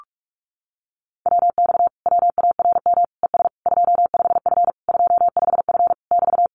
Morse code, heard as a keyed tone, about 44 wpm, 700 Hz. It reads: WX WARM ES 25F 25F <BT>
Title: Common QSO Phrases Encore 45wpm